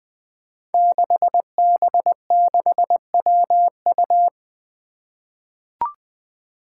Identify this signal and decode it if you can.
Morse code, heard as a keyed tone, about 20 wpm, 700 Hz. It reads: 6B6WU
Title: Five Challenging Letter-Number Combinations 20wpm